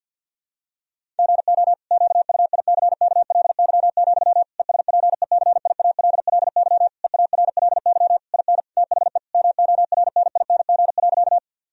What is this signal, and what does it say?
Morse code, heard as a keyed tone, about 50 wpm, 700 Hz. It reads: GO QUICKLY, ESPECIALLY EARLY IN THE MORNING.
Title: Sentences from Top 1000 Words - Review Part 4 - 50wpm